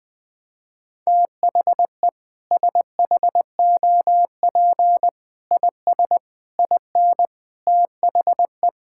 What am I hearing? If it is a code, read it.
Morse code, heard as a keyed tone, about 20 wpm, 700 Hz. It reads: THE SHOP IS IN THE
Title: Sentences with 4-Word Phrases - Spoken Beginning - 20wpm